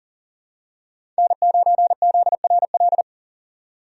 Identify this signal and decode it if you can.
Morse code, heard as a keyed tone, about 40 wpm, 700 Hz. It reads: N9ZRL
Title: USA Call Signs - 5 Characters - 40wpm